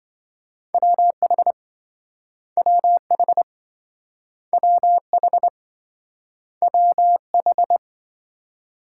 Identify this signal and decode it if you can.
Morse code, heard as a keyed tone, about 27 wpm, 700 Hz. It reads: WH WH WH WH